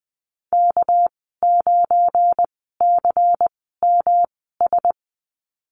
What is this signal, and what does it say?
Morse code, heard as a keyed tone, about 19 wpm, spaced as for 14 wpm, 700 Hz. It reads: K9CMS
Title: 219 Lesson 35g - Number 0 - Call Signs 20wpm